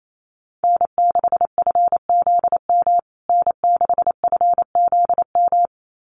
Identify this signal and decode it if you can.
Morse code, heard as a keyed tone, about 28 wpm, 700 Hz. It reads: N6FZM N6FZM
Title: USA Call Signs Repeated 2x 28wpm